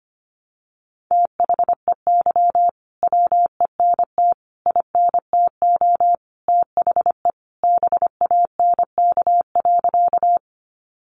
Morse code, heard as a keyed tone, about 25 wpm, 700 Hz. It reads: THEY WENT INTO THE BANK.
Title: Sentences from Top 900 Words 25wpm